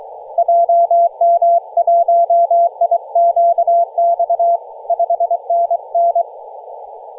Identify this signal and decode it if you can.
Morse code, heard as a keyed tone, about 23 wpm, 680 Hz. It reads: JM1IQX 5NN
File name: posts / 2016-nov-13-0348z-t30jy-28-034mhz-cw